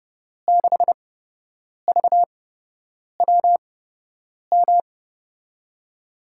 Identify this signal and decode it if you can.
Morse code, heard as a keyed tone, about 30 wpm, spaced as for 8 wpm, 700 Hz. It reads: 6VWM
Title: USA Call Signs - 6 Characters - 8x Character Spacing - 30wpm